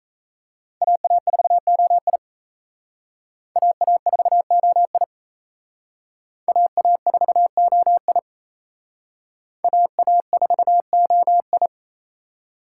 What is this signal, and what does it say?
Morse code, heard as a keyed tone, about 35 wpm, 700 Hz. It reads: AA4OI AA4OI AA4OI AA4OI